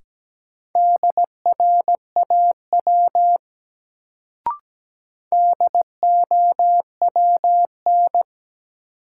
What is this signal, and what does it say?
Morse code, heard as a keyed tone, about 17 wpm, 700 Hz. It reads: DRAW DOWN
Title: Sentences from Top 300 Words - New Word Drill - 17wpm